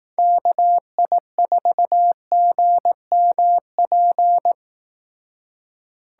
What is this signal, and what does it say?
Morse code, heard as a keyed tone, about 18 wpm, 700 Hz. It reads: KI4GMP
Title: Speed Racing - USA Call Signs 15wpm